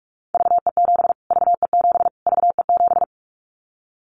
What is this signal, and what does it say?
Morse code, heard as a keyed tone, about 44 wpm, 700 Hz. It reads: VE7 VE7 VE7
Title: Callsign Prefixes for Top 25 DXCC - Repeated 3x - 45wpm